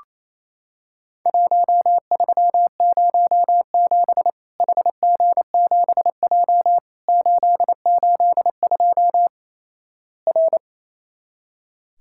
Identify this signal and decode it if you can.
Morse code, heard as a keyed tone, about 28 wpm, 700 Hz. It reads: 1307 HG7J 882 R
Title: Worked All Europe DX Contest - QTC Exchange - 28wpm